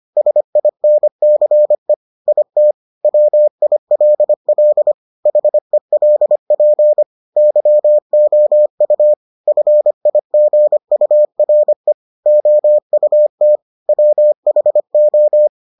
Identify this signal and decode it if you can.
Morse code, heard as a keyed tone, about 25 wpm, 600 Hz. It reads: SINCE IT WILL HELP YOU FIGURE OUT WHO